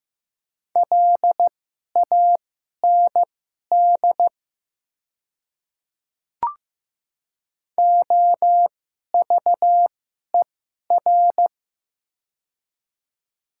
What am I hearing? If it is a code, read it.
Morse code, heard as a keyed tone, about 15 wpm, spaced as for 11 wpm, 700 Hz. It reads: LAND OVER
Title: 2x Character Spacing - Top 300 Words - 15wpm